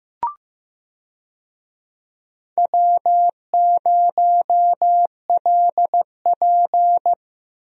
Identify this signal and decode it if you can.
Morse code, heard as a keyed tone, about 15 wpm, 700 Hz. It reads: W0LP
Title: USA Call Signs - 4 Characters - 15wpm